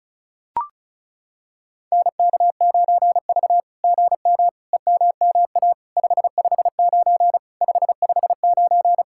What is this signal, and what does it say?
Morse code, heard as a keyed tone, about 36 wpm, 700 Hz. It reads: NK9V GM EMMA 559 559
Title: SOTA QSO Practice 35wpm